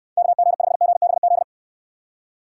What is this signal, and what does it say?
Morse code, heard as a keyed tone, about 60 wpm, 700 Hz. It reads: KG5GBZ